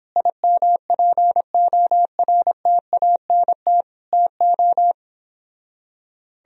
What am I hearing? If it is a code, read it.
Morse code, heard as a keyed tone, about 29 wpm, 700 Hz. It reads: IMPORTANT TO